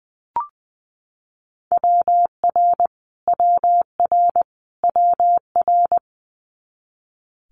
Morse code, heard as a keyed tone, about 20 wpm, 700 Hz. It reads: WR WR WR
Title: Multi-letter Phonemes Repeated 3x 20wpm